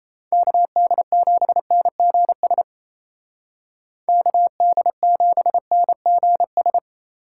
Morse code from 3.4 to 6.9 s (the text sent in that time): KD7NGS